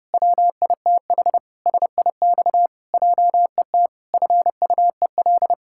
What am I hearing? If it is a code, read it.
Morse code, heard as a keyed tone, about 30 wpm, 700 Hz. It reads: WITH SIX JET FUEL
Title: Pangram Challenge v2 - 30wpm